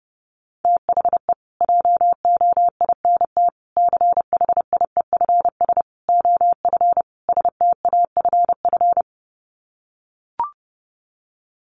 Morse code, heard as a keyed tone, about 28 wpm, 700 Hz. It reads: THE JOINT CHIEFS OF STAFF
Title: Sets of 5 Words Encore2 30wpm